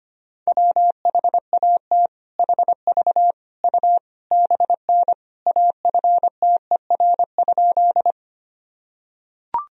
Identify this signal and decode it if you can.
Morse code, heard as a keyed tone, about 25 wpm, 700 Hz. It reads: WHAT HV U BN AFTER?
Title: Sentences from Top 200 Words and Abbreviations 25wpm